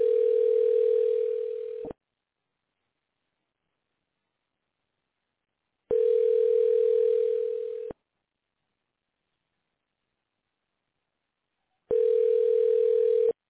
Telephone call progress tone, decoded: ringback tone